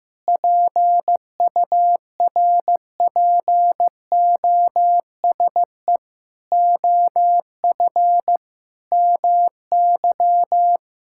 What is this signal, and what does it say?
Morse code, heard as a keyed tone, about 15 wpm, 700 Hz. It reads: PURPOSE OF MY
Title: Sentences from Top 1000 Words - Review Part 6 - 15wpm